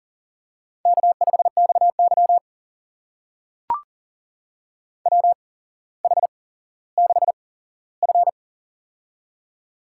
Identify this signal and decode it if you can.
Morse code, heard as a keyed tone, about 42 wpm, spaced as for 17 wpm, 700 Hz. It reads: K5XY WH6F
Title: USA Call Signs - 4 Characters - 8x Character Spacing - 40wpm